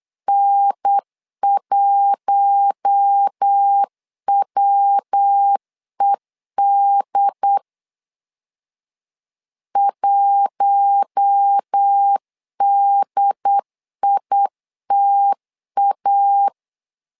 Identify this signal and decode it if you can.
Morse code, heard as a keyed tone, about 8 wpm, 800 Hz. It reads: N1WED 1DITA
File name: LEZIONI / 4 Q1 8wpm 3035